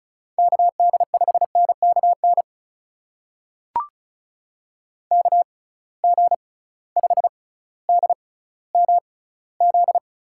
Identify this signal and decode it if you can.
Morse code, heard as a keyed tone, about 36 wpm, spaced as for 18 wpm, 700 Hz. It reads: KD5NKN KG5DMZ
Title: USA Call Signs - 6 Characters - 6x Character Spacing - 35wpm